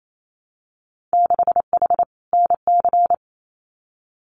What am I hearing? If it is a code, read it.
Morse code, heard as a keyed tone, about 28 wpm, 700 Hz. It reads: 6H NC
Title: Winter Field Day Exchanges Encore 28wpm